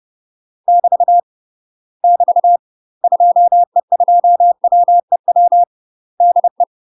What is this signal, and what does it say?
Morse code, heard as a keyed tone, about 30 wpm, 700 Hz. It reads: <BT> <BT> 2E2WEW DE
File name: UK MASTER QSOs TRAINING / 30WPM / UK MASTER QSOs 0036